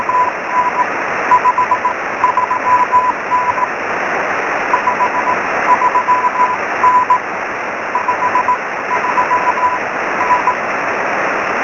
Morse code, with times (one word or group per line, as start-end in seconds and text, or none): none